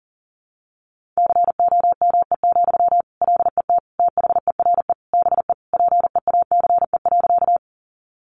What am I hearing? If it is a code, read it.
Morse code, heard as a keyed tone, about 40 wpm, 700 Hz. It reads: COME, LET THERE BE PEACE.
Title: Sentences from Top 1500 Words 40wpm